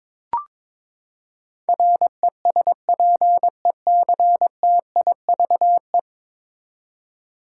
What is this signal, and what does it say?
Morse code, heard as a keyed tone, about 21 wpm, 700 Hz. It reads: RESPECTIVE
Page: morse-code-ninja-practice-22wpm